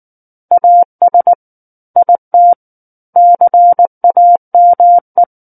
Morse code, heard as a keyed tone, about 19 wpm, 700 Hz. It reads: AS IT CAME
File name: Book One / 10-Chapter 10 - 19 WPM